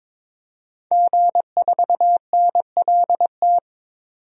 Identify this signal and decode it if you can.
Morse code, heard as a keyed tone, about 22 wpm, 700 Hz. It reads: G4NLT
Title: DXCC-100 Call Signs - 5 Characters - 22wpm